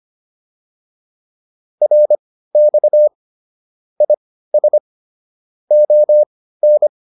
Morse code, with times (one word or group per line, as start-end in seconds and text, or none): 1.8-3.1 RX
4.0-4.8 IS
5.7-6.9 ON